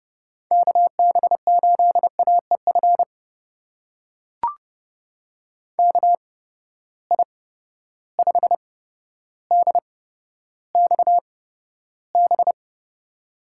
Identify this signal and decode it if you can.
Morse code, heard as a keyed tone, about 30 wpm, spaced as for 13 wpm, 700 Hz. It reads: KB8AEF KI5DXB
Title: USA Call Signs - 6 Characters - 8x Character Spacing - 30wpm